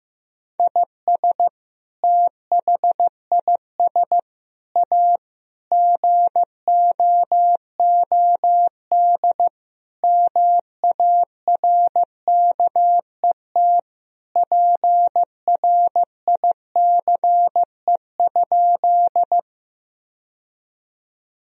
Morse code, heard as a keyed tone, about 15 wpm, 700 Hz. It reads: IS THIS A GOOD MARKET PRICE?
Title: Sentences from Top 1000 Words - Review Part 3 - 15wpm